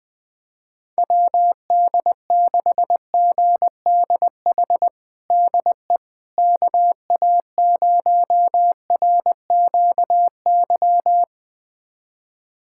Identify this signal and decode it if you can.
Morse code, heard as a keyed tone, about 21 wpm, 700 Hz. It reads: WD6GDH DE KA0RQY
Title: USA Callsign DE Callsign - 6 Characters - 20wpm